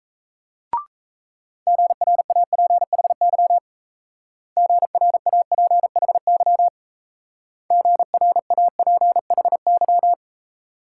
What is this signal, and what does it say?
Morse code, heard as a keyed tone, about 38 wpm, 700 Hz. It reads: GRAPHY GRAPHY GRAPHY